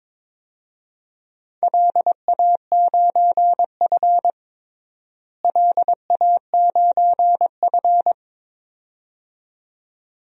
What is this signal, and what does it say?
Morse code, heard as a keyed tone, about 22 wpm, 700 Hz. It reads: LA9F LA9F
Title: DXCC-100 Call Signs - 4 Characters - Repeated 2x - 22wpm